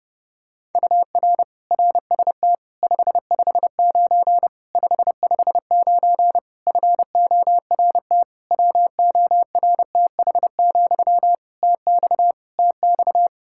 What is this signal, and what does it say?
Morse code, heard as a keyed tone, about 32 wpm, 700 Hz. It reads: UR RST 559 559 FORT WORTH, TX TX
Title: Practice QSO - Exchange 1 - Beginning - 30wpm